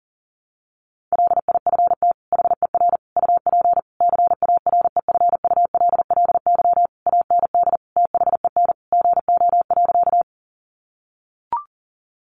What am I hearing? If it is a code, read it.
Morse code, heard as a keyed tone, about 37 wpm, 700 Hz. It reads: LIFT HER UP CAREFULLY AND THEN GO.
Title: Sentences from Top 1300 Words 40wpm